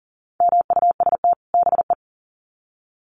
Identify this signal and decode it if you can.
Morse code, heard as a keyed tone, about 40 wpm, 700 Hz. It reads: MUST BE
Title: Sets of 2 Words 40wpm